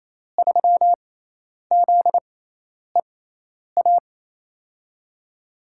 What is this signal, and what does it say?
Morse code, heard as a keyed tone, about 28 wpm, spaced as for 10 wpm, 700 Hz. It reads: 3ZEA
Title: USA Call Signs - 6 Characters - 6x Character Spacing - 28wpm